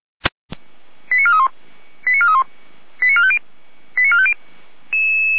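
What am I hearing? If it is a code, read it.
Five-tone selective call (ZVEI), heard as quick runs of five tones, tones of about 70 ms. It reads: 89431 89431 89450 89450
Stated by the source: ZVEI five-tone selective call received on an SDR.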